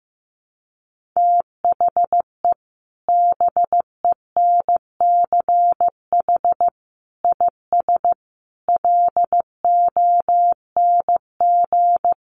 Morse code, heard as a keyed tone, about 15 wpm, 700 Hz. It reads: THE BENCH IS LONG
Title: Sentences from Top 1400 Words 15wpm